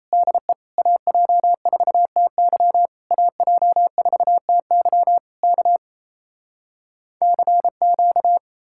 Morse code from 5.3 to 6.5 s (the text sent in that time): K